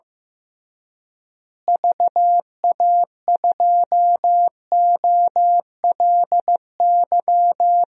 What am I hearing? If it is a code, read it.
Morse code, heard as a keyed tone, about 15 wpm, 700 Hz. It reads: VA2OLY
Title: DXCC-100 Call Signs - 6 Characters - Repeated 2x - 15wpm